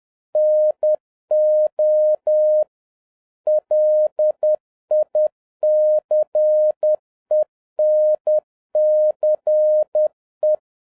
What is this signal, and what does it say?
Morse code, heard as a keyed tone, about 10 wpm, 610 Hz. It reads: NO LICENCE